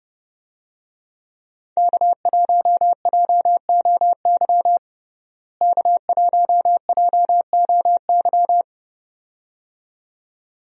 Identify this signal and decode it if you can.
Morse code, heard as a keyed tone, about 31 wpm, 700 Hz. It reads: K1JOY K1JOY
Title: USA Call Signs - 5 Characters - Repeated 2x - 30wpm